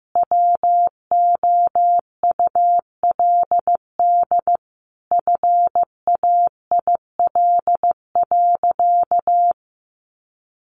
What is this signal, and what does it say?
Morse code, heard as a keyed tone, about 15 wpm, 700 Hz. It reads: WOULD FAIL.
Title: Sentences from Top 1100 Words 15wpm